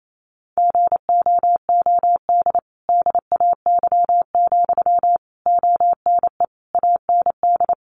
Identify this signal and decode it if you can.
Morse code, heard as a keyed tone, about 28 wpm, 700 Hz. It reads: GOOD DAY, ONE AND
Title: Sentences from Top 200 Words 28wpm